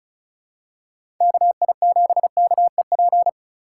Morse code, heard as a keyed tone, about 34 wpm, 700 Hz. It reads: KI7KEP
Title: USA Call Signs - 6 Characters - 4x Character Spacing - 35wpm